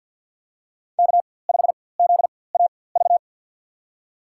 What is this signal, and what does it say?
Morse code, heard as a keyed tone, about 51 wpm, spaced as for 23 wpm, 700 Hz. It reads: K5ZAV
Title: USA Call Signs - 4x Character Spacing - 50wpm